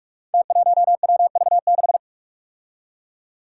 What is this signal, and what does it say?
Morse code, heard as a keyed tone, about 45 wpm, 700 Hz. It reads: T1WV6
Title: Five Challenging Letter-Number Combinations 45wpm